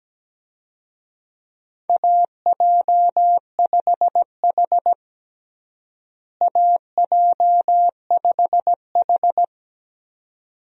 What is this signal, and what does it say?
Morse code, heard as a keyed tone, about 17 wpm, 700 Hz. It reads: AJ5H AJ5H
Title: USA Call Signs - 4 Characters - Repeated 2x - 17wpm